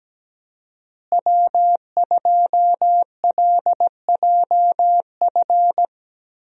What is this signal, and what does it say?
Morse code, heard as a keyed tone, about 17 wpm, 700 Hz. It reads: W2LJF
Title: USA Call Signs - 5 Characters - Repeated 2x - 17wpm